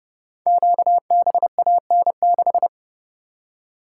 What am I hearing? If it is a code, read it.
Morse code, heard as a keyed tone, about 30 wpm, 700 Hz. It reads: QBAN6
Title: Code Groups - 5 Letter:Numbers - 30wpm